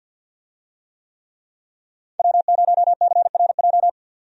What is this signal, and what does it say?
Morse code, heard as a keyed tone, about 52 wpm, 700 Hz. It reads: W0CRJ
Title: USA Call Signs - 5 Characters - 50wpm